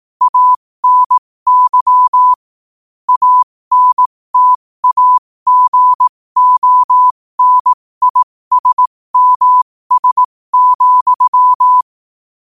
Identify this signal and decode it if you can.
Morse code, heard as a keyed tone, about 18 wpm, spaced as for 15 wpm, 1000 Hz. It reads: ANY ANTAGONISMS,
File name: 15wpm